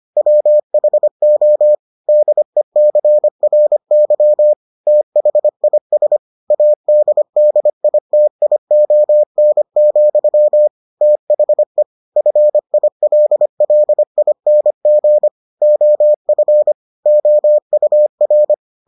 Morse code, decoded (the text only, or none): WHO DECRY THIS ADDITION, THE FILLING OF OUR